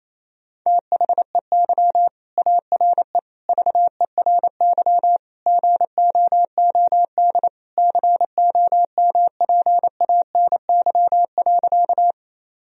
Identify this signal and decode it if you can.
Morse code, heard as a keyed tone, about 27 wpm, 700 Hz. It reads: THEY ARE VERY GOOD COMPANY.
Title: Sentences from Top 1000 Words - Review Part 3 - 28wpm